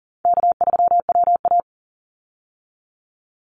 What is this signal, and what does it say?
Morse code, heard as a keyed tone, about 40 wpm, 700 Hz. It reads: K3WA
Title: Easy Call Signs Repeated 3x - 40wpm